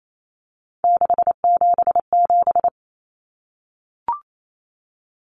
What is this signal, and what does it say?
Morse code, heard as a keyed tone, about 28 wpm, 700 Hz. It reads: 677
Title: Numbers - 3-Digit 28wpm